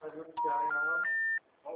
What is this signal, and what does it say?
Special information tone, with speech.
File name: out- -2010- - - .40173